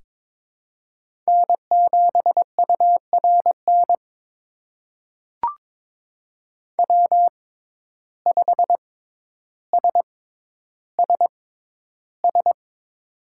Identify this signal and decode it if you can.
Morse code, heard as a keyed tone, about 22 wpm, spaced as for 11 wpm, 700 Hz. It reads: N7URN W5SSS
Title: USA Call Signs - 5 Characters - 6x Character Spacing - 22wpm